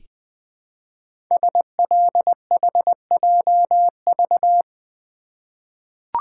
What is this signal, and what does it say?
Morse code, heard as a keyed tone, about 20 wpm, 700 Hz. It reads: SLHJV
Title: Code Groups - 5 Letters - 20wpm